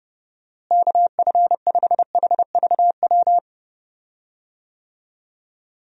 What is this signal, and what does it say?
Morse code, heard as a keyed tone, about 30 wpm, 700 Hz. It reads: KF5HVW
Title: USA Call Signs - 6 Characters - Repeated 3x - 30wpm